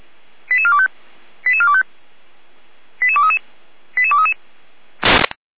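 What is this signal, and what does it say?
Five-tone selective call (ZVEI), heard as quick runs of five tones, tones of about 70 ms. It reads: 88425 88425 88233 88233
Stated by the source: ZVEI five-tone selective call 88393 received on SDR.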